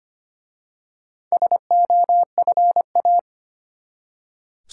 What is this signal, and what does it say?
Morse code, heard as a keyed tone, about 25 wpm, 700 Hz. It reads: SOFA
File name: practice / 25wpm / Easy Words with Hints - 25wpm